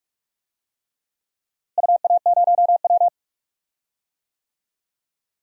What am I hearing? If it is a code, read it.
Morse code, heard as a keyed tone, about 41 wpm, 700 Hz. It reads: UA0W
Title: DXCC-100 Call Signs - 4 Characters - Repeated 3x - 45wpm